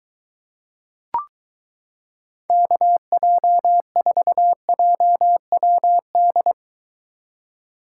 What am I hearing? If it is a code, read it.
Morse code, heard as a keyed tone, about 23 wpm, 700 Hz. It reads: KJ4JWD